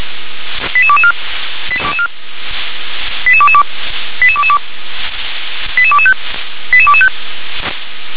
Five-tone selective call (ZVEI), heard as quick runs of five tones, tones of about 70 ms. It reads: 88224 88222 88225 88225